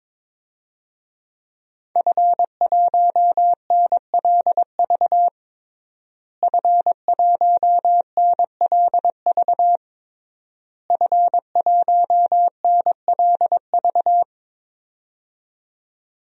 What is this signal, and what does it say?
Morse code, heard as a keyed tone, about 22 wpm, 700 Hz. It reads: F1NLV F1NLV F1NLV
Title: DXCC-100 Call Signs - 5 Characters - Repeated 3x - 22wpm